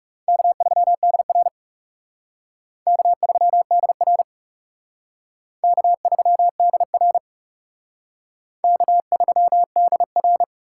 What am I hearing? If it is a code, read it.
Morse code, heard as a keyed tone, about 37 wpm, 700 Hz. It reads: K3DR K3DR K3DR K3DR
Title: Speed Racing - USA 1x2 Call Signs 30wpm